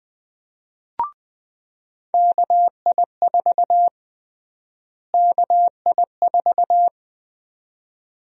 Morse code, heard as a keyed tone, about 20 wpm, 700 Hz. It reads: KI4 KI4
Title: USA Call Sign Prefixes - Repeated 2x - 20wpm